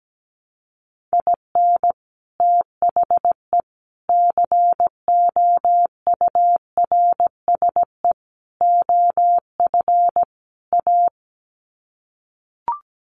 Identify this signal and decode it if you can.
Morse code, heard as a keyed tone, about 17 wpm, 700 Hz. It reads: IN THE COURSE OF A
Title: Sets of 5 Words Encore2 17wpm